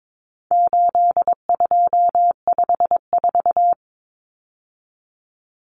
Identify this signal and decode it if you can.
Morse code, heard as a keyed tone, about 22 wpm, 700 Hz. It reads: 8254
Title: Common Cut Numbers - 4-Digit 22wpm